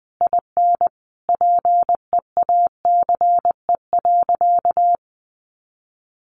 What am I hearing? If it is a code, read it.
Morse code, heard as a keyed tone, about 20 wpm, 700 Hz. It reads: IN PEACE.
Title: Sentences from Top 1500 Words 20wpm